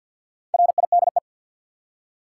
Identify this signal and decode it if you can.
Morse code, heard as a keyed tone, about 52 wpm, 700 Hz. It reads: RIDE
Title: Four Letter Words - 50wpm